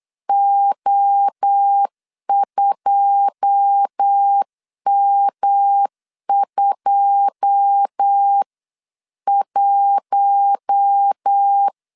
Morse code, heard as a keyed tone, about 8 wpm, 800 Hz. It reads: O2M21
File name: CW / DET / 1 9 1500C S C 8wpm 5653